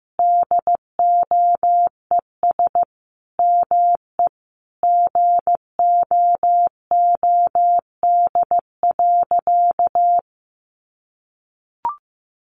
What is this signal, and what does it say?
Morse code, heard as a keyed tone, about 15 wpm, 700 Hz. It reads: DOES ME GOOD.
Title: Sentences from Top 200 Words 15wpm